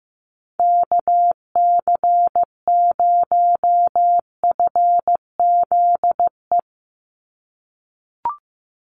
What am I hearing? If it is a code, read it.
Morse code, heard as a keyed tone, about 15 wpm, 700 Hz. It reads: KC0FZE
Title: USA Call Signs 15wpm